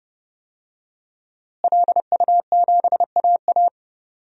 Morse code, heard as a keyed tone, about 29 wpm, 700 Hz. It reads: LU7AA
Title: DXCC-100 Call Signs - 5 Characters - Repeated 3x - 30wpm